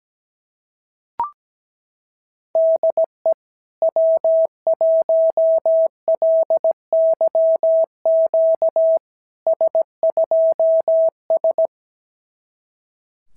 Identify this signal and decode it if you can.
Morse code, heard as a keyed tone, about 17 wpm, 650 Hz. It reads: DE W1LYQ S2S
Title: SOTA QSO Practice 17wpm